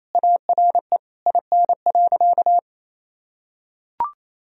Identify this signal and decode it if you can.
Morse code, heard as a keyed tone, about 28 wpm, 700 Hz. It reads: ARE IN.
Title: Sentences from Top 1000 Words - Review Part 2 - 28wpm